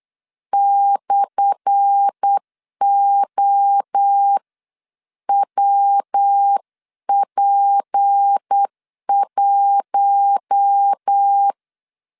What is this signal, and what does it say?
Morse code, heard as a keyed tone, about 8 wpm, spaced as for 7 wpm, 800 Hz. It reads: /OWP1
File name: LEZIONI / 10 SIGNS / 10A SLASH 8wpm 3452